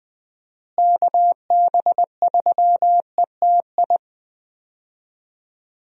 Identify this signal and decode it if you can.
Morse code, heard as a keyed tone, about 20 wpm, 700 Hz. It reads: KB3ETI
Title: USA Call Signs - 6 Characters - Repeated 3x - 20wpm